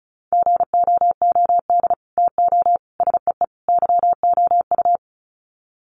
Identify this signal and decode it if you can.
Morse code, heard as a keyed tone, about 34 wpm, 700 Hz. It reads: GOOD TO SEE YOU
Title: Sets of 4 Words 35wpm